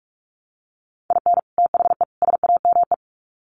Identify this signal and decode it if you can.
Morse code, heard as a keyed tone, about 44 wpm, 700 Hz. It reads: IN THE SAME
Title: Sets of 3 Words 45wpm